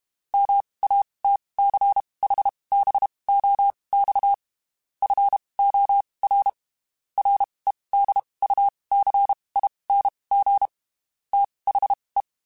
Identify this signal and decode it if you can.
Morse code, heard as a keyed tone, about 32 wpm, spaced as for 23 wpm, 800 Hz. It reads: MATCHBOX FOR REDUCING THE
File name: (spaced) / QSOs Spaced-mp3s / WD0FBT 1 @30 sp